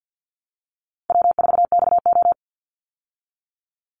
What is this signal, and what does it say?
Morse code, heard as a keyed tone, about 50 wpm, 700 Hz. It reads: W4XO
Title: Easy Call Signs Repeated 3x - 50wpm